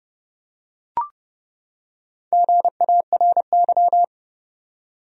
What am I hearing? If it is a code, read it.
Morse code, heard as a keyed tone, about 30 wpm, 700 Hz. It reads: GARY
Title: USA Common Names - Max 4 Letters - 30wpm